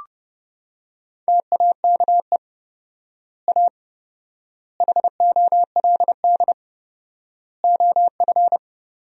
Morse code, heard as a keyed tone, about 32 wpm, 700 Hz. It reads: TAKE A HOLD OF
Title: 4x Word-Spacing - Sentences from Top 400 Words - 30wpm